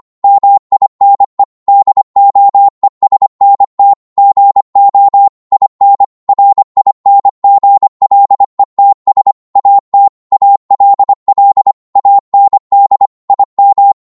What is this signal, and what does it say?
Morse code, heard as a keyed tone, about 25 wpm, 800 Hz. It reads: MINE DOESNT GO IN RINGLETS AT ALL AND IM